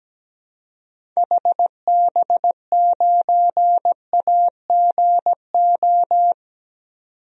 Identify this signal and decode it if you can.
Morse code, heard as a keyed tone, about 17 wpm, 700 Hz. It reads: HB9AGO
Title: DXCC-100 Call Signs - 6 Characters - 17wpm